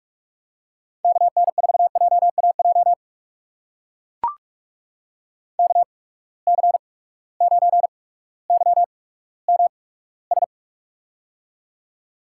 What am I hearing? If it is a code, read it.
Morse code, heard as a keyed tone, about 43 wpm, spaced as for 19 wpm, 700 Hz. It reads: KN4JAJ KC9YMS
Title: USA Call Signs - 6 Characters - 8x Character Spacing - 45wpm